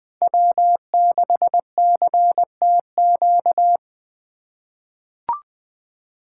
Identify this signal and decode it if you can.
Morse code, heard as a keyed tone, about 21 wpm, 700 Hz. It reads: W6CTQ
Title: USA Call Signs - 5 Characters - 3x Character Spacing - 20wpm